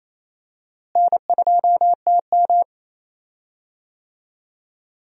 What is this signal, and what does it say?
Morse code, heard as a keyed tone, about 28 wpm, 700 Hz. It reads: N2TM
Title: USA Call Signs - 4 Characters - Repeated 3x - 28wpm